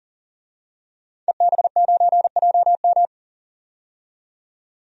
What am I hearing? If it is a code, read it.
Morse code, heard as a keyed tone, about 40 wpm, 700 Hz. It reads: EB9JM
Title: DXCC-100 Call Signs - 5 Characters - 40wpm